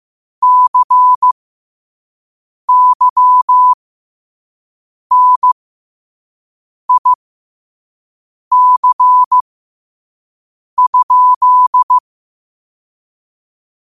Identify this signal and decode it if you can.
Morse code, heard as a keyed tone, about 15 wpm, spaced as for 5 wpm, 1000 Hz. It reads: CYNIC?